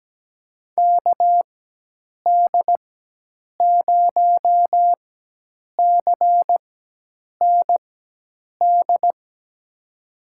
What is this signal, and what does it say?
Morse code, heard as a keyed tone, about 17 wpm, spaced as for 8 wpm, 700 Hz. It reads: KD0CND
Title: USA Call Signs - 4x Character Spacing - 17wpm